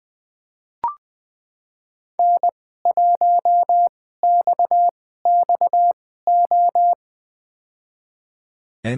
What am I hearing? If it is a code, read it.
Morse code, heard as a keyed tone, about 20 wpm, spaced as for 14 wpm, 700 Hz. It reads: N1XXO
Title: USA Call Signs - 5 Characters - 2x Character Spacing - 20wpm